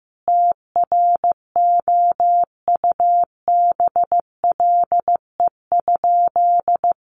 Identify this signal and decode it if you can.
Morse code, heard as a keyed tone, about 15 wpm, 700 Hz. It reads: TROUBLE?
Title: Sentences from Top 1200 Words 15wpm